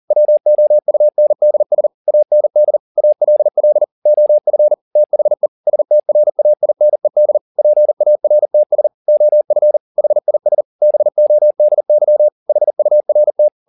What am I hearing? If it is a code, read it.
Morse code, heard as a keyed tone, about 40 wpm, 600 Hz. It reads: WOUNDS AND ALL OF THE STRAINED PARTS OF HIS BODY HURT
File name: the old man and the sea 2 40wpm